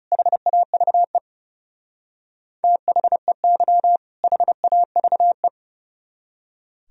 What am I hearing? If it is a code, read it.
Morse code, heard as a keyed tone, about 40 wpm, 700 Hz. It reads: HAVE THEY HAVE